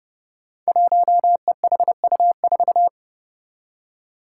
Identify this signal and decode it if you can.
Morse code, heard as a keyed tone, about 28 wpm, 700 Hz. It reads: 1EHU4
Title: Five Challenging Letter-Number Combinations 30wpm